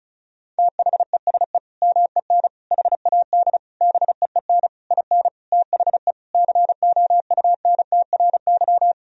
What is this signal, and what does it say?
Morse code, heard as a keyed tone, about 36 wpm, 700 Hz. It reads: THESE MEN HAD BEEN IN THE COUNTRY
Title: Sentences from Top 1000 Words - Review Part 1 - 35wpm